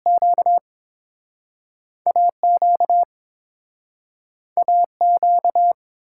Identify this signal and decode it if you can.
Morse code, heard as a keyed tone, about 27 wpm, spaced as for 5 wpm, 700 Hz. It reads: QA QA Q
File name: Speed-Racing - Two Letter-Number Combinations - 22wpm